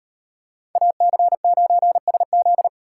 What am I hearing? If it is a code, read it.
Morse code, heard as a keyed tone, about 38 wpm, 700 Hz. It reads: AC9SZ